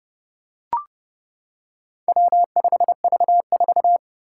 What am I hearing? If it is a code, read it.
Morse code, heard as a keyed tone, about 30 wpm, 700 Hz. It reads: W5V4
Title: Four Challenging Letter-Number Combinations 30wpm